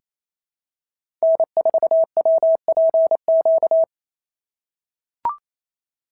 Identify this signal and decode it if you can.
Morse code, heard as a keyed tone, about 27 wpm, 650 Hz. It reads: N4WPQ
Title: SOTA QSO Practice 28wpm